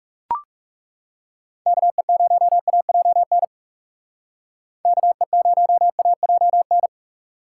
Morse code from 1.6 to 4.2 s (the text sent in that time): KE0AJN